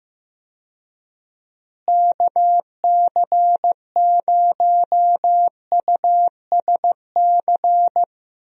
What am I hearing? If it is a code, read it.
Morse code, heard as a keyed tone, about 16 wpm, 700 Hz. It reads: KC0USC
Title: USA Call Signs - 6 Characters - Repeated 3x - 15wpm